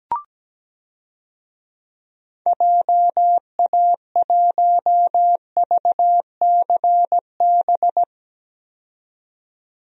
Morse code, heard as a keyed tone, about 17 wpm, 700 Hz. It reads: JA1VCB
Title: DXCC-100 Call Signs - 6 Characters - 17wpm